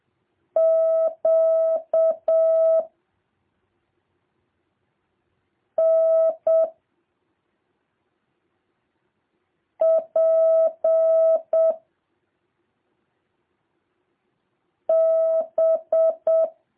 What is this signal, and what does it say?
Morse code, heard as a keyed tone, about 7 wpm, 650 Hz. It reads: Q N P B